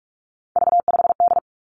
Morse code, heard as a keyed tone, about 46 wpm, 700 Hz. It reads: V5D
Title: Three Challenging Letter-Number Combinations 45wpm